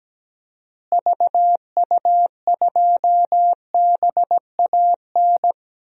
Morse code, heard as a keyed tone, about 17 wpm, 700 Hz. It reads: VU2BAN
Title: DXCC-100 Call Signs - 6 Characters - Repeated 3x - 17wpm